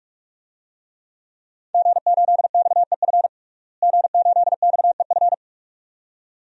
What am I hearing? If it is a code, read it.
Morse code, heard as a keyed tone, about 46 wpm, 700 Hz. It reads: G8XEF G8XEF
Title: DXCC-100 Call Signs - 5 Characters - Repeated 2x - 45wpm